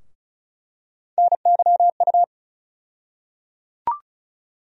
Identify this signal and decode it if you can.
Morse code, heard as a keyed tone, about 34 wpm, 700 Hz. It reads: NYU
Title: USA Call Sign Postfixes - 3 Letters 35wpm